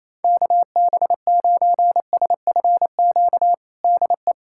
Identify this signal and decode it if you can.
Morse code, heard as a keyed tone, about 28 wpm, 700 Hz. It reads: KB9SFQ DE
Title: USA Callsign DE Callsign - 6 Characters - 28wpm